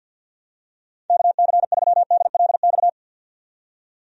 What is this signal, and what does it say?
Morse code, heard as a keyed tone, about 50 wpm, 700 Hz. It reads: KC3DLX
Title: USA Call Signs - 6 Characters - 50wpm